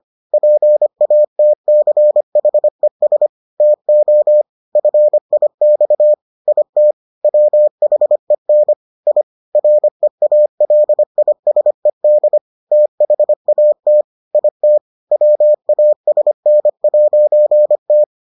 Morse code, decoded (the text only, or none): PATCHES TO FIX IT WHEN I REALISED THAT IT WASN'T